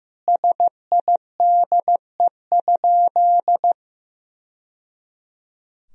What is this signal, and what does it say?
Morse code, heard as a keyed tone, about 15 wpm, 700 Hz. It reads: SIDE?
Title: Sentences from Top 1000 Words - Review Part 3 - 15wpm